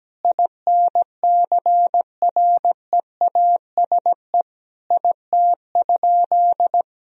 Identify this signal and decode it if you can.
Morse code, heard as a keyed tone, about 17 wpm, 700 Hz. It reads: INCREASE IT?
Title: Sentences from Top 1000 Words - Review Part 4 - 17wpm